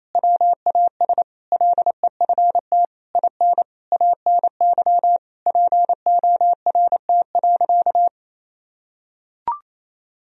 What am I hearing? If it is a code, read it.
Morse code, heard as a keyed tone, about 28 wpm, 700 Hz. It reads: WAS LEFT IN ANY PORT.
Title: Sentences from Top 1000 Words - Review Part 2 - 28wpm